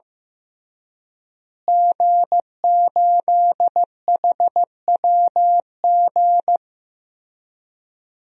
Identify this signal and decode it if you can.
Morse code, heard as a keyed tone, about 16 wpm, 700 Hz. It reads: G8HWG
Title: DXCC-100 Call Signs - 5 Characters - Repeated 3x - 15wpm